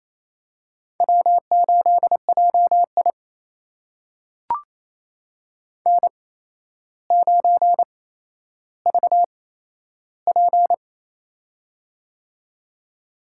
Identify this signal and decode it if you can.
Morse code, heard as a keyed tone, about 28 wpm, spaced as for 12 wpm, 700 Hz. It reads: W8JI N9VP
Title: USA Call Signs - 4 Characters - 8x Character Spacing - 28wpm